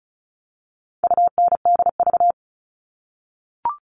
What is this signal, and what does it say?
Morse code, heard as a keyed tone, about 34 wpm, 700 Hz. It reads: UNDV
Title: All Cut Numbers - 4-Digit 35wpm